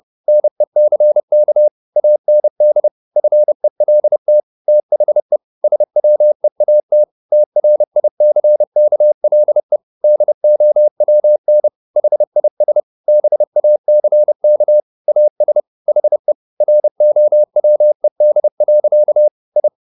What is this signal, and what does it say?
Morse code, heard as a keyed tone, about 30 wpm, 600 Hz. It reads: NECK AND FELT THE SWEAT TRICKLE DOWN HIS BACK AS HE ROWED. I
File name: the old man and the sea / the old man and the sea 1 30wpm